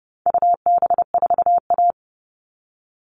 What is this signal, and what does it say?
Morse code, heard as a keyed tone, about 30 wpm, 700 Hz. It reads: UB4A
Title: All Cut Numbers - 4-Digit 30wpm